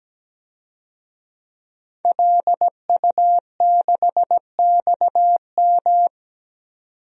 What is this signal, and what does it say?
Morse code, heard as a keyed tone, about 17 wpm, 700 Hz. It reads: LU6XM
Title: DXCC-100 Call Signs - 5 Characters - 17wpm